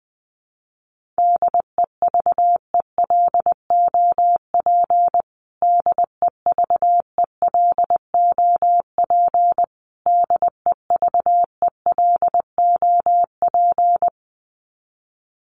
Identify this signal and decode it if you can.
Morse code, heard as a keyed tone, about 20 wpm, 700 Hz. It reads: DEVELOP DEVELOP DEVELOP
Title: Top 500 Words Repeated 20wpm